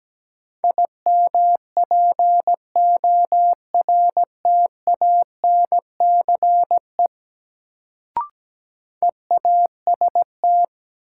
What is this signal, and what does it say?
Morse code, heard as a keyed tone, about 17 wpm, 700 Hz. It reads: IMPORTANCE EAST
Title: Sentences from Top 1200 Words - New Word Drill Alt - 17wpm